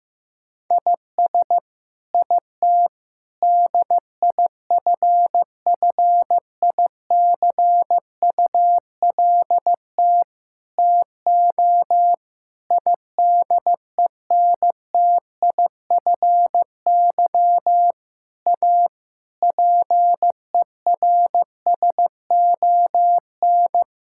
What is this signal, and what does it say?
Morse code, heard as a keyed tone, about 16 wpm, 700 Hz. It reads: IS IT DIFFICULT TO IDENTIFY A PERSON
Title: Sentences from Top 1000 Words - Review Part 6 - 15wpm